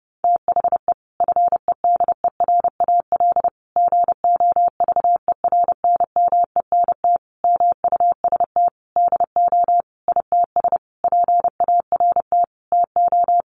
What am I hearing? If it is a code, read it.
Morse code, heard as a keyed tone, about 30 wpm, 700 Hz. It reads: THE FEDERAL GOVERNMENT MUST DO ITS PART TO
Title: Sentences from Top 1100 Words 30wpm